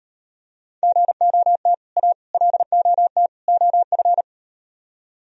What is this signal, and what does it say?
Morse code, heard as a keyed tone, about 38 wpm, 700 Hz. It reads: GOT A LOT OF